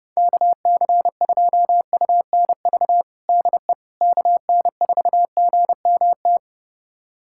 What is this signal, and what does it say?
Morse code, heard as a keyed tone, about 30 wpm, 700 Hz. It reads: KC2UNV DE KN4GMT
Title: USA Callsign DE Callsign - 6 Characters - 30wpm